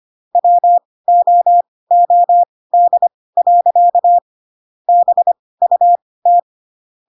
Morse code, heard as a keyed tone, about 25 wpm, spaced as for 18 wpm, 700 Hz. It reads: WOOD. BUT
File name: Pinocchio / Pinocchio ch1 12 25 ik1hge